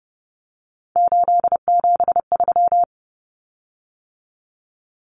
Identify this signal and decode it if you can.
Morse code, heard as a keyed tone, about 30 wpm, 700 Hz. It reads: 873
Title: Common Cut Numbers - 3-Digit 30wpm